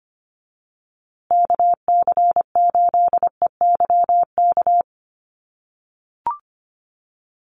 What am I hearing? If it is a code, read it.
Morse code, heard as a keyed tone, about 25 wpm, 700 Hz. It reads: KC8EYK
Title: USA Call Signs Repeated 2x 25wpm